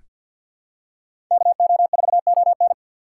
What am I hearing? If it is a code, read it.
Morse code, heard as a keyed tone, about 47 wpm, 700 Hz. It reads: KO4ON
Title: USA Call Signs - 5 Characters - 6x Character Spacing - 50wpm